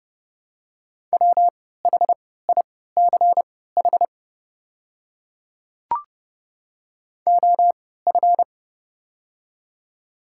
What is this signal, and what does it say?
Morse code, heard as a keyed tone, about 30 wpm, spaced as for 17 wpm, 690 Hz. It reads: WHICH OF
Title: 3x Character Spacing - Top 100 Words - 30wpm